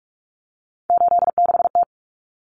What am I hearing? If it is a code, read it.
Morse code, heard as a keyed tone, about 44 wpm, 700 Hz. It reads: 86T
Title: Common Cut Numbers - 3-Digit 45wpm